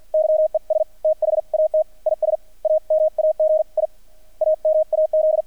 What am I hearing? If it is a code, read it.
Morse code, heard as a keyed tone, about 48 wpm, 630 Hz. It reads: YES THAT IS AMAG I AMAZ
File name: archived / CRSnet